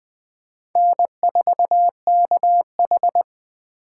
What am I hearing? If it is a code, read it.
Morse code, heard as a keyed tone, about 20 wpm, 700 Hz. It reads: N4KH
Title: K1USN SST QSO Practice - 20wpm